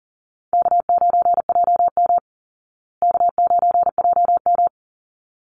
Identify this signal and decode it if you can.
Morse code, heard as a keyed tone, about 40 wpm, 700 Hz. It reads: K9JM K9JM
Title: Easy Call Signs Repeated 2x - 40wpm